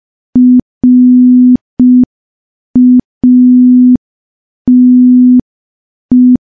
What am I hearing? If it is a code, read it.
Morse code, heard as a keyed tone, about 5 wpm, 250 Hz. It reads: RATE
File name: morse / morse-training-05WPM